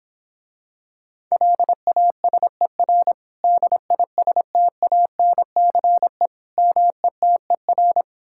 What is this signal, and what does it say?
Morse code, heard as a keyed tone, about 26 wpm, 700 Hz. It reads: LASER DISTANCE METER